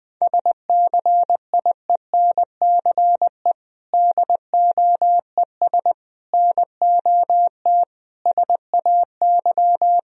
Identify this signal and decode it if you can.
Morse code, heard as a keyed tone, about 20 wpm, 700 Hz. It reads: SCIENCE DOES NOT SAY
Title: Sentences from Top 1000 Words - Review Part 2 - 20wpm